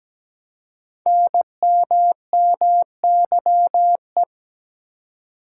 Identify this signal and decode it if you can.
Morse code, heard as a keyed tone, about 17 wpm, 700 Hz. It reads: NMMYE
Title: Code Groups - 5 Letters - 17wpm